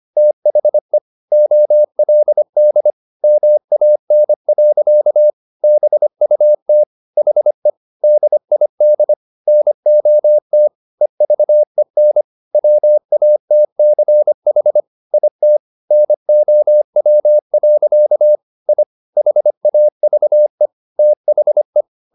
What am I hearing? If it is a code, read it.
Morse code, heard as a keyed tone, about 25 wpm, 600 Hz. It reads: THE OLD MAN. BUT HE DID NOT EVEN WATCH IT NOW. I HAVE THE